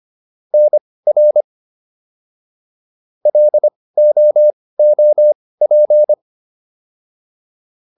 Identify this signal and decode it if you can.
Morse code, heard as a keyed tone, about 25 wpm, spaced as for 18 wpm, 600 Hz. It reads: NR LOOP